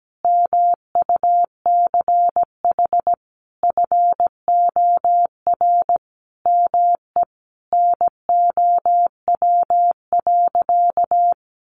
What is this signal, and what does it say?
Morse code, heard as a keyed tone, about 17 wpm, 700 Hz. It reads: MUCH FOR ME NOW.
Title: Sentences from Top 200 Words 17wpm